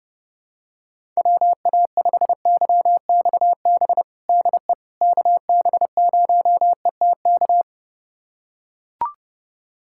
Morse code, heard as a keyed tone, about 32 wpm, 700 Hz. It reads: WA5YXB DE KB0ETK
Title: USA Callsign DE Callsign - 6 Characters - 30wpm